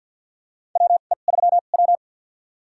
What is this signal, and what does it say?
Morse code, heard as a keyed tone, about 53 wpm, spaced as for 37 wpm, 690 Hz. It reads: WE3W
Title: USA Call Signs - 2x Character Spacing - 50wpm